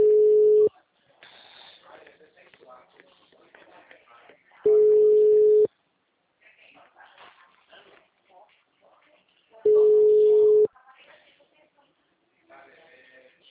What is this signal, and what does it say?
Ringback tone (Europe, 425 Hz), with speech.
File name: out- -2018- - - .72167